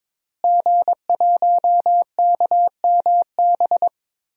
Morse code, heard as a keyed tone, about 22 wpm, 700 Hz. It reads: G1KMB
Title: DXCC-100 Call Signs - 5 Characters - Repeated 2x - 22wpm